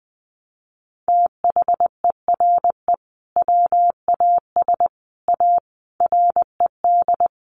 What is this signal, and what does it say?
Morse code, heard as a keyed tone, about 20 wpm, 700 Hz. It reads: THERE WAS A RED
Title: Sentences from Top 700 Words 20wpm